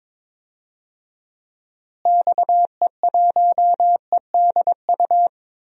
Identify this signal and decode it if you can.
Morse code, heard as a keyed tone, about 22 wpm, 700 Hz. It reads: XE1EDU
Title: DXCC-100 Call Signs - 6 Characters - Repeated 3x - 22wpm